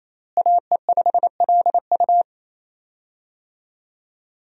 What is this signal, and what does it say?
Morse code, heard as a keyed tone, about 28 wpm, 700 Hz. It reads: AE5LU
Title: USA Call Signs - 5 Characters - Repeated 3x - 28wpm